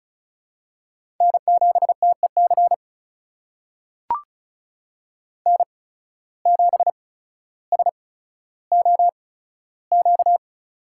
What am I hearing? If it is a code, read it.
Morse code, heard as a keyed tone, about 36 wpm, spaced as for 15 wpm, 700 Hz. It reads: N7TEC N7SOQ
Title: USA Call Signs - 8x Character Spacing - 35wpm